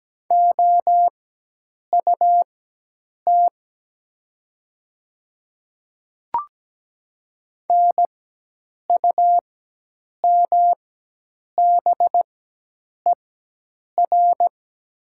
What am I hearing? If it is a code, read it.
Morse code, heard as a keyed tone, about 17 wpm, spaced as for 8 wpm, 700 Hz. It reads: OUT NUMBER
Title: 4x Character Spacing - Top 500 Words - 17wpm